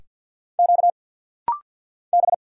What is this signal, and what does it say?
Morse code, heard as a keyed tone, about 50 wpm, 700 Hz. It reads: Q B
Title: Single Letters - Rapid-Fire 50wpm